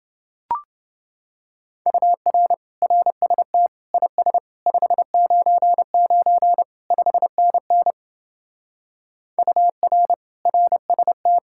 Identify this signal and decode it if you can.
Morse code, heard as a keyed tone, about 27 wpm, 700 Hz. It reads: UR RST IS 599 5NN UR RST